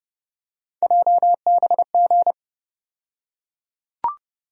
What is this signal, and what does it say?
Morse code, heard as a keyed tone, about 30 wpm, 700 Hz. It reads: JBG
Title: USA Call Sign Postfixes - 3 Letters 30wpm